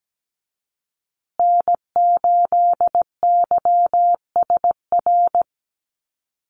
Morse code, heard as a keyed tone, about 17 wpm, 700 Hz. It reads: N8YSR
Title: USA Call Signs Repeated 3x 17wpm